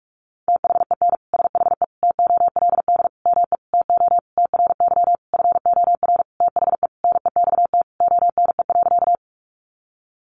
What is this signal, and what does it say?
Morse code, heard as a keyed tone, about 44 wpm, 700 Hz. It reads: THEN SHE TOLD ME TO TRY FOR THE NEXT ONE.
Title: Sentences from Top 400 Words 45wpm